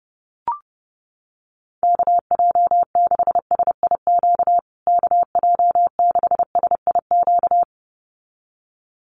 Morse code, heard as a keyed tone, about 30 wpm, 700 Hz. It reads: KJ6SIQ KJ6SIQ
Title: USA Call Signs Repeated 2x 30wpm